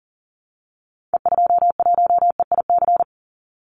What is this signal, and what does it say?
Morse code, heard as a keyed tone, about 40 wpm, 700 Hz. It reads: E21EIC
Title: Call Signs Repeated 3x - 40wpm